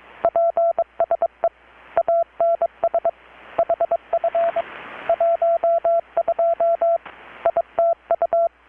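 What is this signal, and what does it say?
Morse code, heard as a keyed tone, about 22 wpm, 670 Hz. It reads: PSE ANS HF 12 ITU
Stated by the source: Morse from KPH's Night of Nights XX maritime radiotelegraphy event, received in Chiapas, Mexico about 3,800 km away on a Kenwood R-600 with a grounded 10 m long-wire antenna.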